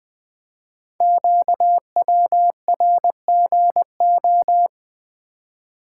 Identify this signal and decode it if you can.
Morse code, heard as a keyed tone, about 20 wpm, 700 Hz. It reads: QWRGO
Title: Code Groups - 5 Letters - 20wpm